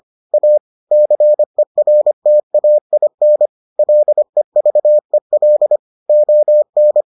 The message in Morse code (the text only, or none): A CERTAIN LEVEL ON